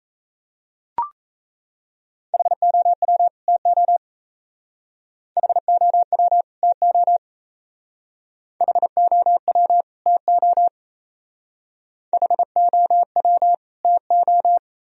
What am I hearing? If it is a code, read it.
Morse code, heard as a keyed tone, about 34 wpm, 700 Hz. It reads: HOW TO HOW TO HOW TO HOW TO
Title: Speed-Racing - Sets of 2 Words 28wpm